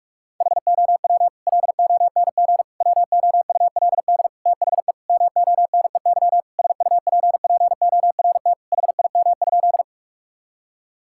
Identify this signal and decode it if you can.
Morse code, heard as a keyed tone, about 46 wpm, 700 Hz. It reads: HOW LONG WOULD THE MONEY SUPPORT HIM?
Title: Sentences from Top 1000 Words - Review Part 5 - 45wpm